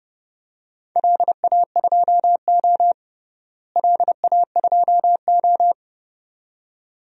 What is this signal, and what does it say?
Morse code, heard as a keyed tone, about 32 wpm, 700 Hz. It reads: LA2O LA2O
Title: DXCC-100 Call Signs - 4 Characters - Repeated 2x - 30wpm